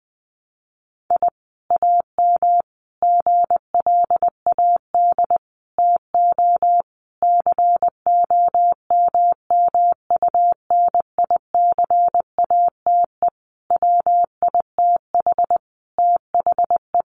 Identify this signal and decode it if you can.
Morse code, heard as a keyed tone, about 19 wpm, 700 Hz. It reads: I AM GLAD TO COMMUNICATE WITH THE
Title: Sentences from Top 1300 Words 20wpm